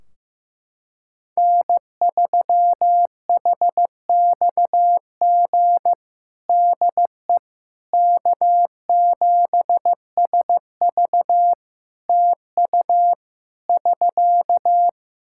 Morse code, heard as a keyed tone, about 15 wpm, 700 Hz. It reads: N3HXG DE K7SV TU <SK>
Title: K1USN SST QSO Practice 15wpm